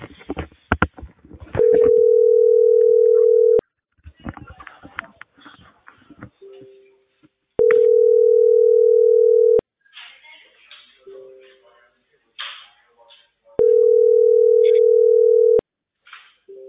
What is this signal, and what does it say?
Ringback tone (North America), 440 + 480 Hz, with speech.
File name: outgoing recordings